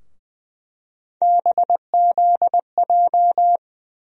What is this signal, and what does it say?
Morse code, heard as a keyed tone, about 20 wpm, 700 Hz. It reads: BZJ
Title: USA Call Sign Postfixes - 3 Letters 20wpm